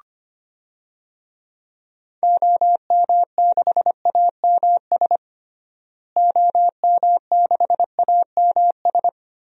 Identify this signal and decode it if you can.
Morse code, heard as a keyed tone, about 25 wpm, 700 Hz. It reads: OM6AMS OM6AMS
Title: DXCC-100 Call Signs - 6 Characters - Repeated 2x - 25wpm